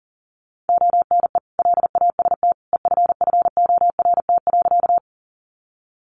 Morse code, heard as a keyed tone, about 40 wpm, 700 Hz. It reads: ONE LAST EFFORT.
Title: Sentences from Top 1200 Words 40wpm